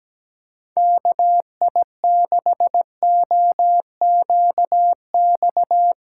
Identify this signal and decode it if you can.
Morse code, heard as a keyed tone, about 17 wpm, 700 Hz. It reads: KI6OQX
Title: USA Callsign DE Callsign 17wpm